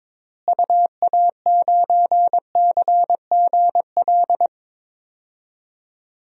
Morse code, heard as a keyed tone, about 22 wpm, 700 Hz. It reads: UA9CGL
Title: DXCC-100 Call Signs - 6 Characters - Repeated 3x - 22wpm